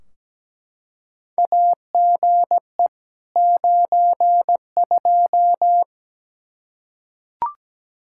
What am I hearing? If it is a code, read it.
Morse code, heard as a keyed tone, about 17 wpm, 700 Hz. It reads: AGE 92
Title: Practice QSO - Exchange 3 - More Info - 17wpm